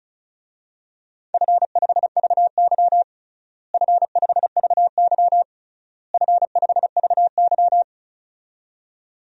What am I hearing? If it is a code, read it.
Morse code, heard as a keyed tone, about 35 wpm, 700 Hz. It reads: F5VY F5VY F5VY
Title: DXCC-100 Call Signs - 4 Characters - Repeated 3x - 35wpm